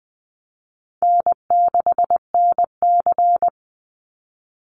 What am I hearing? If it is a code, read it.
Morse code, heard as a keyed tone, about 20 wpm, 700 Hz. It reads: N6NC
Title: Easy Call Signs Repeated 2x - 20wpm